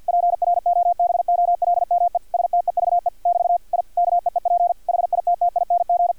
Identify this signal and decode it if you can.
Morse code, heard as a keyed tone, about 50 wpm, 700 Hz, with radio noise. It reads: PROBOLME STEVE <BT> I KEEP HITTING
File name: archived / CRSnet